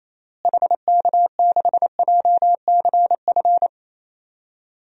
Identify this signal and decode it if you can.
Morse code, heard as a keyed tone, about 28 wpm, 700 Hz. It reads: HK6JCF
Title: DXCC-100 Call Signs - 6 Characters - Repeated 2x - 28wpm